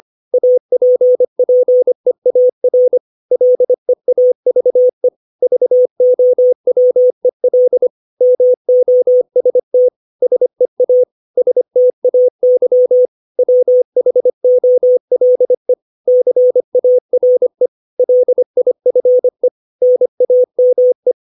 Morse code, heard as a keyed tone, about 25 wpm, 500 Hz. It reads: APPEAR LEAVE VOWEL MOST SEA STAY WHOLE CARE LIFE NAME
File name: 632 words / 632 Words 25WPM